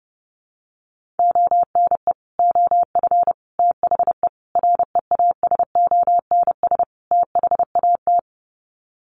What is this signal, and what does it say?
Morse code, heard as a keyed tone, about 30 wpm, 700 Hz. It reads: ONE OF THE REASONS THAT
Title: Sets of 5 Words Encore 30wpm